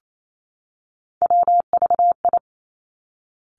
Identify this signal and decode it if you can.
Morse code, heard as a keyed tone, about 28 wpm, 700 Hz. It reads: WVI
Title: Three Challenging Letter-Number Combinations 28wpm